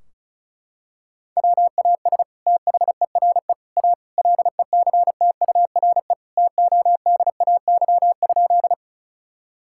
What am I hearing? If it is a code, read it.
Morse code, heard as a keyed tone, about 34 wpm, 700 Hz. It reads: WAS THERE A LECTURE TODAY?
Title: Sentences from Top 1600 Words 35wpm